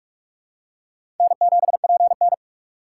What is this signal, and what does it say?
Morse code, heard as a keyed tone, about 46 wpm, 700 Hz. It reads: N7PN
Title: USA Call Signs - 6x Character Spacing - 45wpm